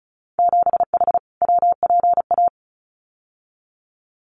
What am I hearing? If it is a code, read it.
Morse code, heard as a keyed tone, about 36 wpm, 700 Hz. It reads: 7H WPA
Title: Winter Field Day Exchanges Encore 35wpm